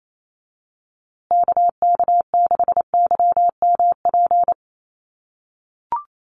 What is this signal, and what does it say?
Morse code, heard as a keyed tone, about 28 wpm, 700 Hz. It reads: KK6YMP
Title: USA Call Signs Repeated 3x 28wpm